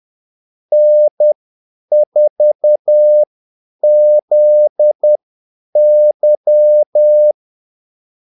Morse code, heard as a keyed tone, about 10 wpm, spaced as for 8 wpm, 600 Hz. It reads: N4ZY
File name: Callsigns-10-8wpm